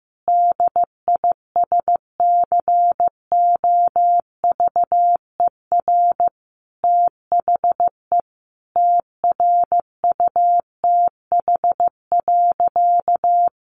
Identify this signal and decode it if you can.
Morse code, heard as a keyed tone, about 14 wpm, 700 Hz. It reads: DISCOVER THE TRUTH.
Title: Sentences from Top 1100 Words 15wpm